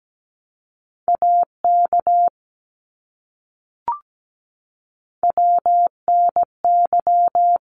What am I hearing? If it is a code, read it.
Morse code, heard as a keyed tone, about 17 wpm, 700 Hz. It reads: AK WNY
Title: ARRL Sections 17wpm